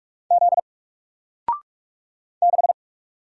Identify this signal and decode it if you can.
Morse code, heard as a keyed tone, about 44 wpm, 700 Hz. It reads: Z 6
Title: Single Letter-Number 45wpm